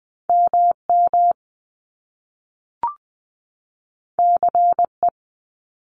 Morse code, heard as a keyed tone, about 20 wpm, 700 Hz. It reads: MM CE
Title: Top Two Letter Combinations in Words 20wpm